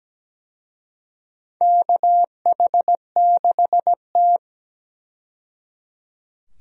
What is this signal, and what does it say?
Morse code, heard as a keyed tone, about 17 wpm, 700 Hz. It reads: KH6T
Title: USA Call Signs - 4 Characters - 17wpm